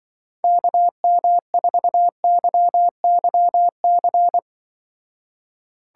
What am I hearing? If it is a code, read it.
Morse code, heard as a keyed tone, about 25 wpm, 700 Hz. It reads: KM4YYC